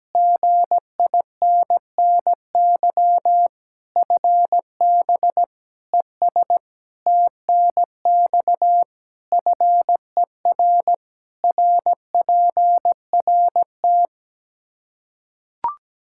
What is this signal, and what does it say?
Morse code, heard as a keyed tone, about 17 wpm, 700 Hz. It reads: GINNY FB ES TNX FER RPRT
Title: Practice QSO - Exchange 2 - Info - 17wpm